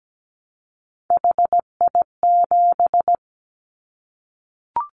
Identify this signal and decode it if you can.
Morse code, heard as a keyed tone, about 17 wpm, 700 Hz. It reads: HI7
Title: Callsign Prefixes for Top 100 DXCC - 17wpm